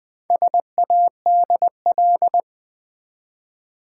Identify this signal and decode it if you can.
Morse code, heard as a keyed tone, about 20 wpm, 700 Hz. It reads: SADL
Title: Build A Word - 1 Letter At A Time - 3 to 5-Letter Words - 20wpm